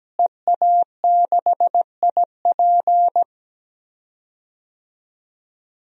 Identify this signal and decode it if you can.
Morse code, heard as a keyed tone, about 17 wpm, 700 Hz. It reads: EA6IP
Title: DXCC-100 Call Signs - 5 Characters - Repeated 3x - 17wpm